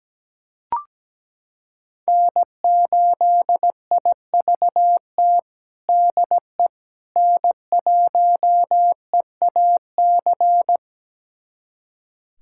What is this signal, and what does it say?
Morse code, heard as a keyed tone, about 17 wpm, 700 Hz. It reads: N8IVT DE N1EAC
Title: USA Callsign DE Callsign - 5 Characters - 17wpm